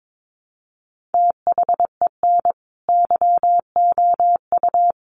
Morse code, heard as a keyed tone, about 22 wpm, 700 Hz. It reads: THEN YOU
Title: Sentences from Top 600 Words 22wpm